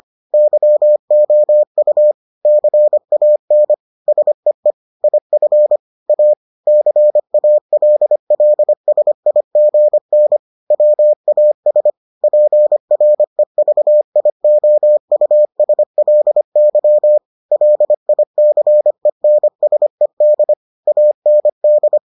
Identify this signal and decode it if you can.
Morse code, heard as a keyed tone, about 25 wpm, 600 Hz. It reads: YOU CAN SEE IF A CALLSIGN WAS PREVIOUSLY LICENSED AND